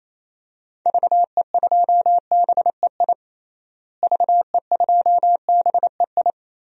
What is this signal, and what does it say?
Morse code, heard as a keyed tone, about 27 wpm, 700 Hz. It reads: VE2BEI VE2BEI
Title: DXCC-100 Call Signs - 6 Characters - Repeated 3x - 28wpm